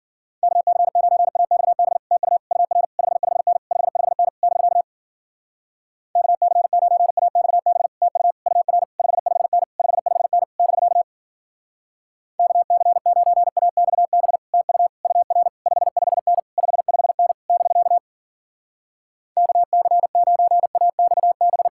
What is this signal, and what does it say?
Morse code, heard as a keyed tone, about 51 wpm, 700 Hz. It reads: KC9AXB TU UR 55N 55N <BK> KC9AXB TU UR 55N 55N <BK> KC9AXB TU UR 55N 55N <BK> KC9AXB